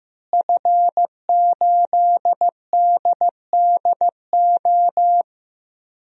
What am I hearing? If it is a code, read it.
Morse code, heard as a keyed tone, about 15 wpm, 700 Hz. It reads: F8DDO
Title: DXCC-100 Call Signs - 5 Characters - Repeated 3x - 15wpm